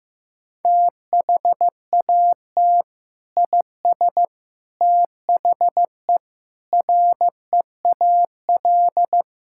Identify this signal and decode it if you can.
Morse code, heard as a keyed tone, about 15 wpm, 700 Hz. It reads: THAT IS THE REAL
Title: Sentences from Top 1000 Words - Review Part 6 - 15wpm